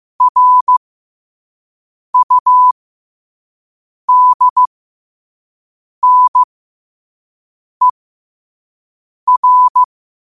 Morse code, heard as a keyed tone, about 15 wpm, spaced as for 5 wpm, 1000 Hz. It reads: RUDNER